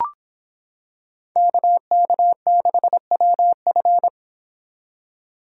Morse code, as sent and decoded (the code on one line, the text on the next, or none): -.- -.- -.... .-- ..-.
KK6WF